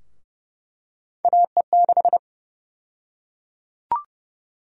Morse code, heard as a keyed tone, about 30 wpm, 700 Hz. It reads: AE6
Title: USA Call Sign Prefixes - 30wpm